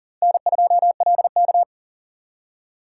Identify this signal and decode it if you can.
Morse code, heard as a keyed tone, about 40 wpm, 700 Hz. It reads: N2LK
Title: K1USN SST QSO Practice - 40wpm